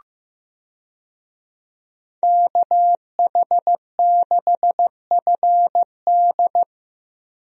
Morse code, heard as a keyed tone, about 15 wpm, 700 Hz. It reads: KH6FD
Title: USA Call Signs - 5 Characters - 15wpm